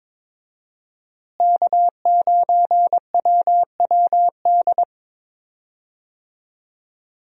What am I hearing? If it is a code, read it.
Morse code, heard as a keyed tone, about 22 wpm, 700 Hz. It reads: K9WWD
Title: USA Call Signs - 5 Characters - Repeated 3x - 22wpm